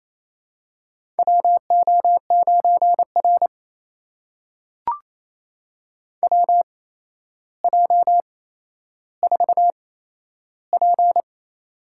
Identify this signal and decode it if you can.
Morse code, heard as a keyed tone, about 28 wpm, spaced as for 12 wpm, 700 Hz. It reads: WO9R WJ4P
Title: USA Call Signs - 4 Characters - 8x Character Spacing - 28wpm